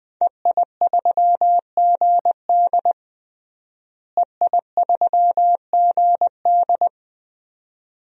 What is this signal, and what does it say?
Morse code, heard as a keyed tone, about 20 wpm, 700 Hz. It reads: EI3GD EI3GD
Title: DXCC-100 Call Signs - 5 Characters - Repeated 2x - 20wpm